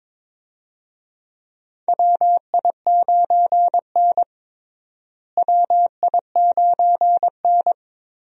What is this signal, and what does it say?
Morse code, heard as a keyed tone, about 22 wpm, 700 Hz. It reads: WI9N WI9N
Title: USA Call Signs - 4 Characters - Repeated 3x - 22wpm